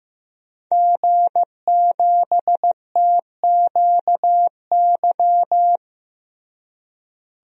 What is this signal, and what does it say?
Morse code, heard as a keyed tone, about 16 wpm, 700 Hz. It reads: G7TQY
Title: DXCC-100 Call Signs - 5 Characters - Repeated 3x - 15wpm